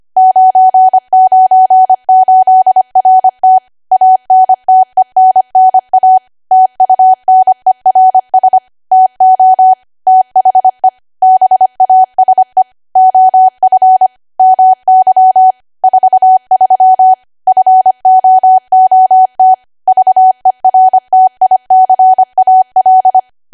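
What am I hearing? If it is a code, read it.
Morse code, heard as a keyed tone, about 25 wpm, 750 Hz. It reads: 998RT ANTENNA TUNERS TO THE BASE OF MY 43 FOOT VERTICAL
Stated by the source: ARRL 25 WPM code practice archive.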